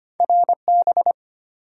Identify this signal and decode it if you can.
Morse code, heard as a keyed tone, about 25 wpm, 700 Hz. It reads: RB
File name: Two Letter-Number Combinations - 25wpm